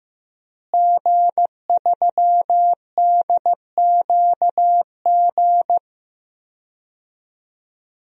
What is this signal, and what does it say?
Morse code, heard as a keyed tone, about 15 wpm, 700 Hz. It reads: G3DQG
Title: DXCC-100 Call Signs - 5 Characters - Repeated 3x - 15wpm